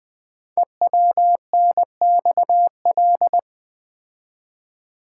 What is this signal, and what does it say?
Morse code, heard as a keyed tone, about 20 wpm, 700 Hz. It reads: EWNXL
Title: Code Groups - 5 Letters - 20wpm